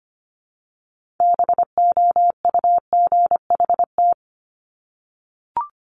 Morse code, heard as a keyed tone, about 25 wpm, 700 Hz. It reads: BOUGHT
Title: Multi-letter Phonemes Repeated 3x 25wpm